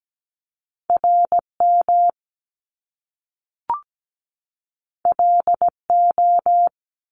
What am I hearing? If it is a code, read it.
Morse code, heard as a keyed tone, about 17 wpm, 700 Hz. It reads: RM LO
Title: Top Two Letter Combinations in Words 17wpm